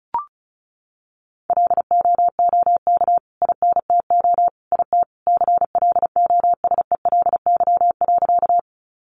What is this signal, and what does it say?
Morse code, heard as a keyed tone, about 34 wpm, 700 Hz. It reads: LOOK INTO IT CLOSELY.
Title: Sentences from Top 1400 Words 35wpm